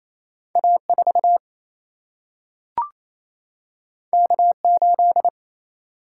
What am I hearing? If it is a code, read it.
Morse code, heard as a keyed tone, about 28 wpm, 700 Hz. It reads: A 4 K 8
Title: Single-Letter Single-Number 28wpm